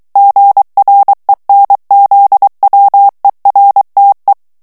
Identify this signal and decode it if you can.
Morse code, heard as a keyed tone, about 23 wpm, 800 Hz. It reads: GRENZWERTE
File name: morse mp3 / ALLEMAND texte clair 1 vitesse 22 mots minute